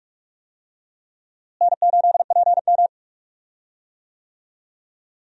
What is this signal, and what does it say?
Morse code, heard as a keyed tone, about 46 wpm, 700 Hz. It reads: N8PM
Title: USA Call Signs - 4 Characters - Repeated 2x - 45wpm